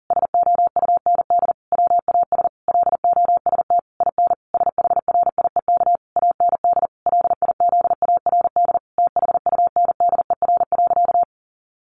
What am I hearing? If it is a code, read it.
Morse code, heard as a keyed tone, about 40 wpm, 700 Hz. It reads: SOUND WAS LOST IN SHRIEK AND LIZARD THUNDER.
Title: A Sound Of Thunder 40wpm